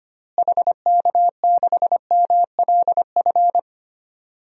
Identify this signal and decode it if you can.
Morse code, heard as a keyed tone, about 25 wpm, 700 Hz. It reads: HK6MLF
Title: DXCC-100 Call Signs - 6 Characters - Repeated 2x - 25wpm